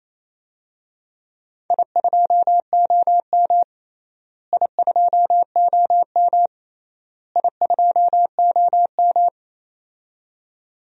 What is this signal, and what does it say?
Morse code, heard as a keyed tone, about 27 wpm, 700 Hz. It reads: I2OM I2OM I2OM
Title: DXCC-100 Call Signs - 4 Characters - Repeated 3x - 28wpm